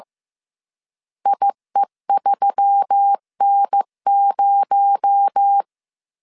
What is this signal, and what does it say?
Morse code, heard as a keyed tone, about 15 wpm, 800 Hz. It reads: IE3N0
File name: CW / DET / 1 6 1500C S C 14wpm 2951